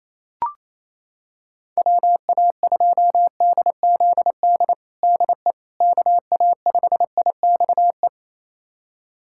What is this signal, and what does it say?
Morse code, heard as a keyed tone, about 27 wpm, 700 Hz. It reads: WA2DZD DE KA5IXE
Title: USA Callsign DE Callsign - 6 Characters - 28wpm